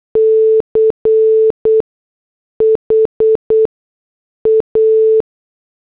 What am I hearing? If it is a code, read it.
Morse code, heard as a keyed tone, about 8 wpm, spaced as for 6 wpm, 440 Hz. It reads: CHA